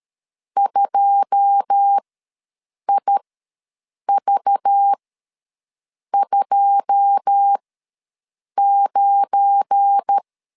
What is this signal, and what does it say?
Morse code, heard as a keyed tone, about 13 wpm, spaced as for 6 wpm, 800 Hz. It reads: 2IV29
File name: LEZIONI / 9 2 ws 12wpm 3406